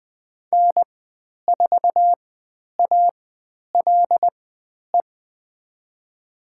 Morse code, heard as a keyed tone, about 20 wpm, spaced as for 10 wpm, 700 Hz. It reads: N4ALE
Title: 109 Lesson 18g - Letter m - Call Signs 20wpm